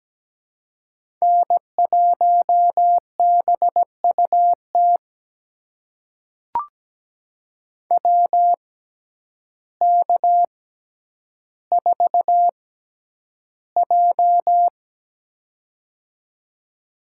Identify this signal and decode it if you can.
Morse code, heard as a keyed tone, about 17 wpm, spaced as for 9 wpm, 700 Hz. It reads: N1BUT WK4J
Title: USA Call Signs - 6x Character Spacing - 17wpm